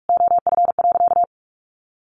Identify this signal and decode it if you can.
Morse code, heard as a keyed tone, about 44 wpm, 700 Hz. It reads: OF.
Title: Sentences from Top 1300 Words 45wpm